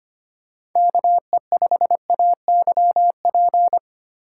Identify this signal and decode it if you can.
Morse code, heard as a keyed tone, about 25 wpm, 700 Hz. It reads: KE5AYP
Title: USA Call Signs - 4x Character Spacing - 25wpm